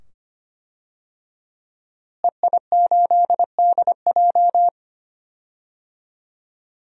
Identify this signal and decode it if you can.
Morse code, heard as a keyed tone, about 25 wpm, 700 Hz. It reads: EI8DJ
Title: DXCC-100 Call Signs - 5 Characters - Repeated 2x - 25wpm